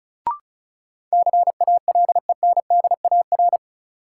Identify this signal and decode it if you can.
Morse code, heard as a keyed tone, about 34 wpm, 700 Hz. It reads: CALENDAR
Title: Sentences from Top 1600 Words - New Word Drill - 35wpm